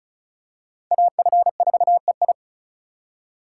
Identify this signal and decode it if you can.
Morse code, heard as a keyed tone, about 36 wpm, 700 Hz. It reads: AF4EI
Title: USA Call Signs - 5 Characters - 2x Character Spacing - 35wpm